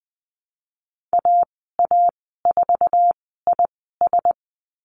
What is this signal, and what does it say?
Morse code, heard as a keyed tone, about 19 wpm, spaced as for 14 wpm, 700 Hz. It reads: AA4IS
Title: 088 Lesson 15g - Number 5 - Call Signs 20wpm